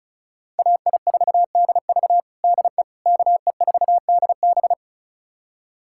Morse code, heard as a keyed tone, about 36 wpm, 700 Hz. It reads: AI4DV DE KE4DB
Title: USA Callsign DE Callsign - 5 Characters - 35wpm